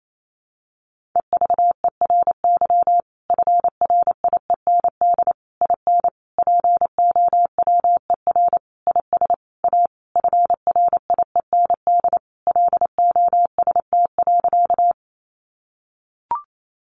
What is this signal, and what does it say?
Morse code, heard as a keyed tone, about 28 wpm, 700 Hz. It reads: EVERY FRIEND IN POWER IS A FRIEND LOST.
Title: Sentences from Top 1400 Words 28wpm